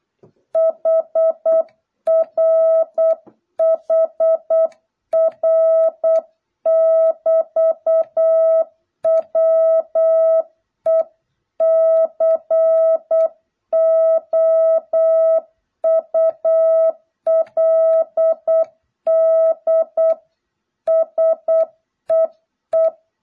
Morse code, heard as a keyed tone, about 8 wpm, 650 Hz. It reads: HR HR <BT> WE COULD SEE